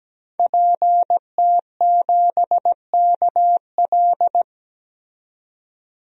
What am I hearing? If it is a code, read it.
Morse code, heard as a keyed tone, about 17 wpm, 700 Hz. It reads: PT7KL
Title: DXCC-100 Call Signs - 5 Characters - 17wpm